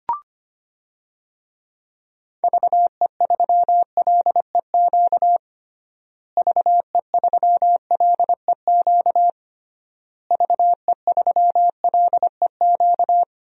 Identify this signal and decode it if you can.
Morse code, heard as a keyed tone, about 25 wpm, 700 Hz. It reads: VE3LEQ VE3LEQ VE3LEQ
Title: DXCC-100 Call Signs - 6 Characters - Repeated 3x - 25wpm